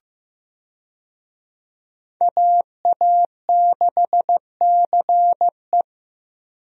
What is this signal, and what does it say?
Morse code, heard as a keyed tone, about 15 wpm, 700 Hz. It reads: AA6CE
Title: USA Call Signs - 5 Characters - Repeated 3x - 15wpm